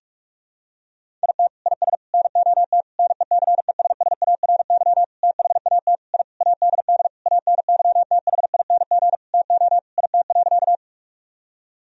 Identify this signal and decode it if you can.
Morse code, heard as a keyed tone, about 46 wpm, 700 Hz. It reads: IT IS NOT NECESSARY THAT I ADD ANYTHING TO IT.
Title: Sentences from Top 1000 Words - Review Part 1 - 45wpm